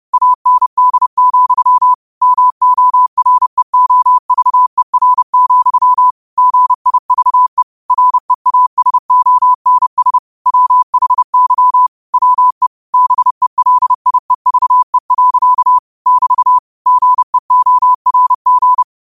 Morse code, heard as a keyed tone, about 30 wpm, 1000 Hz. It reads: AND, MOREOVER, GIVE REASONS WHY WE BELIEVE. <BT> GEORG
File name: qotdfiles / 30wpm